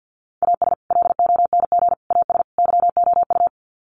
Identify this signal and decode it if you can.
Morse code, heard as a keyed tone, about 50 wpm, 700 Hz. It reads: AS LONG AS YOU
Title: Sets of 4 Words 50wpm